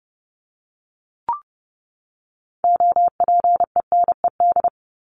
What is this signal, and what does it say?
Morse code, heard as a keyed tone, about 30 wpm, 700 Hz. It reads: OPENED
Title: ED Suffix - Top 100 Words with Base Words 30wpm